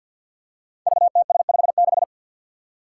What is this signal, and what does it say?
Morse code, heard as a keyed tone, about 46 wpm, 700 Hz. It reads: VTS56
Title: Five Challenging Letter-Number Combinations 50wpm